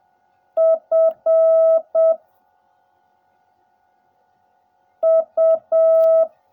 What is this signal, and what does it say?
Morse code, heard as a keyed tone, about 7 wpm, 650 Hz. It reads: F U
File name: jh8cbh / 2021cw / cw16a